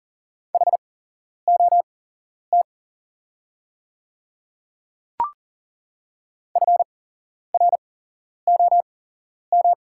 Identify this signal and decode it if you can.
Morse code, heard as a keyed tone, about 40 wpm, spaced as for 11 wpm, 700 Hz. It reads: HOT FROM
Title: 8x Character Spacing - Top 100 Words - 40wpm